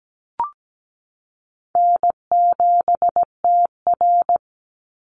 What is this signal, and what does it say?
Morse code, heard as a keyed tone, about 17 wpm, 700 Hz. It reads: N7TR
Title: Call Signs Repeated 2x - 17wpm